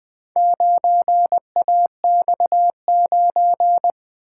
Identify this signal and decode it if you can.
Morse code, heard as a keyed tone, about 21 wpm, 700 Hz. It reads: 9AX9
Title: Speed-Racing - Four Letter-Number Combinations - 20wpm